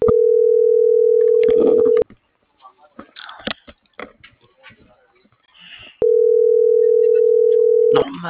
Ringback tone (North America), 440 + 480 Hz.